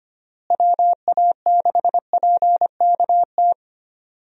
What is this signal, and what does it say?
Morse code, heard as a keyed tone, about 25 wpm, 700 Hz. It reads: WA6PKT
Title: USA Call Signs - 6 Characters - Repeated 2x - 25wpm